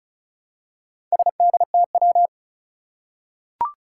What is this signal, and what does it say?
Morse code, heard as a keyed tone, about 36 wpm, 700 Hz. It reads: SDTW
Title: Four Challenging Letter-Number Combinations 35wpm